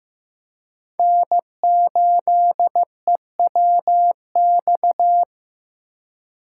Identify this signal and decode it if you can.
Morse code, heard as a keyed tone, about 15 wpm, 700 Hz. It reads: N8EWX
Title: K1USN SST QSO Practice - 15wpm